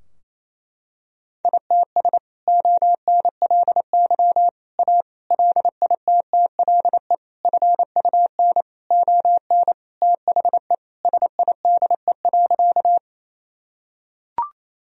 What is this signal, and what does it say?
Morse code, heard as a keyed tone, about 27 wpm, 700 Hz. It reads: ITS ONLY A LITTLE FUN ON THE SIDE.
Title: Sentences from Top 1000 Words - Review Part 4 - 28wpm